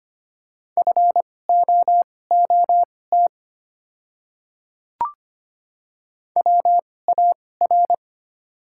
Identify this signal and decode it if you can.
Morse code, heard as a keyed tone, about 25 wpm, spaced as for 18 wpm, 700 Hz. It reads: FOOT WAR
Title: 2x Character Spacing - Top 500 Words - 25wpm